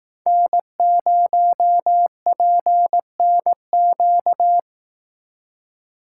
Morse code, heard as a keyed tone, about 18 wpm, 700 Hz. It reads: N0PNQ
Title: Speed Racing - USA 1x3 Call Signs 15wpm